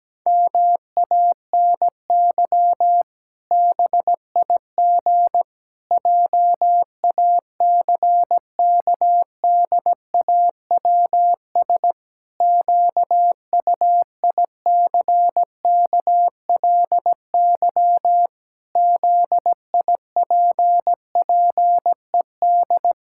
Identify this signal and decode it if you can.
Morse code, heard as a keyed tone, about 17 wpm, 700 Hz. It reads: MANY BIG JACKDAWS QUICKLY ZIPPED
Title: Pangram Challenge v2 - 17wpm